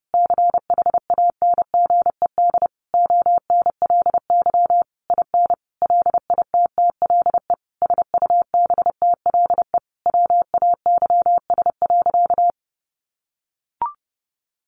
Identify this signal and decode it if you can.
Morse code, heard as a keyed tone, about 30 wpm, 700 Hz. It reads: CHANGED ONLY IN LITTLE SUBTLE WAYS.
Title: A Sound Of Thunder 30wpm